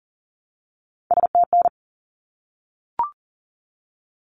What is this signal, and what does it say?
Morse code, heard as a keyed tone, about 40 wpm, 700 Hz. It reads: STN
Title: Abbreviations 40wpm